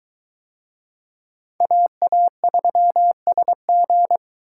Morse code, heard as a keyed tone, about 20 wpm, 700 Hz. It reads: AA3SG